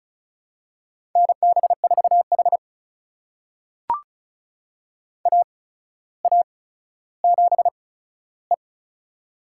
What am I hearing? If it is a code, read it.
Morse code, heard as a keyed tone, about 36 wpm, spaced as for 15 wpm, 700 Hz. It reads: NB4H AA7E
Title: USA Call Signs - 4 Characters - 8x Character Spacing - 35wpm